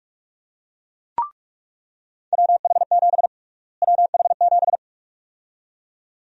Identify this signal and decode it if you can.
Morse code, heard as a keyed tone, about 41 wpm, 700 Hz. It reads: WH7 WH7
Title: USA Call Sign Prefixes - Repeated 2x - 45wpm